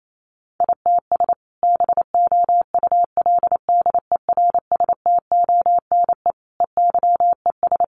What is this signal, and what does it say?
Morse code, heard as a keyed tone, about 28 wpm, 700 Hz. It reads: ITS BOULDERSTONE EYES
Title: A Sound Of Thunder 28wpm